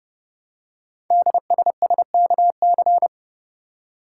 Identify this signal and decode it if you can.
Morse code, heard as a keyed tone, about 30 wpm, 700 Hz. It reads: DSSKC
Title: Code Groups - 5 Letters - 30wpm